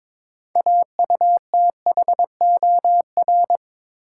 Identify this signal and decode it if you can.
Morse code, heard as a keyed tone, about 22 wpm, 700 Hz. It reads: AUTHOR
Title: Sets of 4 Words - New Word Drill - 22wpm